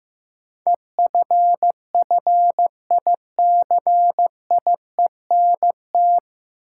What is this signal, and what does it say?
Morse code, heard as a keyed tone, about 15 wpm, 700 Hz. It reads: EFFICIENT
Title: Sentences from Top 1300 Words - New Word Drill - 15wpm